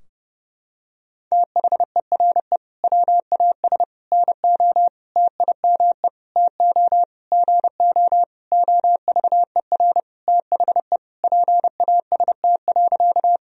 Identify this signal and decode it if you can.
Morse code, heard as a keyed tone, about 30 wpm, 700 Hz. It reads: THERE WAS NO TIME TO GO OVER THE PAST.
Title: Sentences from Top 1000 Words - Review Part 3 - 30wpm